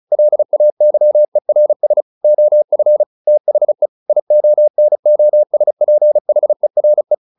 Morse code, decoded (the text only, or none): LAYERS OF THE IONOSPHERE